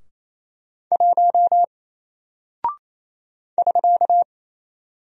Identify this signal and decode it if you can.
Morse code, heard as a keyed tone, about 27 wpm, 700 Hz. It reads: 1 <SK>
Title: Single Character (Letter:Number:Punctuation:Prosign) Alt 28wpm